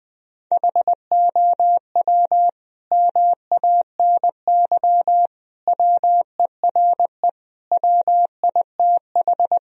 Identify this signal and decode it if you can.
Morse code, heard as a keyed tone, about 20 wpm, 700 Hz. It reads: HOW MANY WERE WITH
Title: Sentences from Top 1000 Words - Review Part 1 - 20wpm